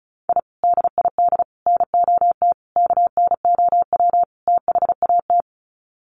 Morse code, heard as a keyed tone, about 34 wpm, 700 Hz. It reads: I DID NOT KNOW THAT
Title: Sets of 5 Words 35wpm